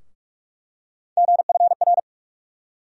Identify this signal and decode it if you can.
Morse code, heard as a keyed tone, about 48 wpm, 700 Hz. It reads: GFR
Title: USA Call Sign Postfixes - 3 Letters 45wpm